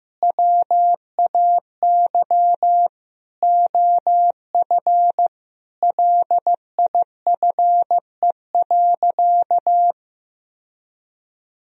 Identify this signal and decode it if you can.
Morse code, heard as a keyed tone, about 15 wpm, 700 Hz. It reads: WAY OF LIFE.
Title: Sentences from Top 1000 Words - Review Part 3 - 15wpm